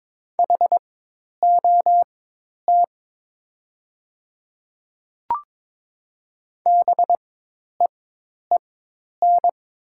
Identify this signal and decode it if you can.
Morse code, heard as a keyed tone, about 22 wpm, spaced as for 10 wpm, 700 Hz. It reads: HOT BEEN
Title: 4x Character Spacing - Top 100 Words - 22wpm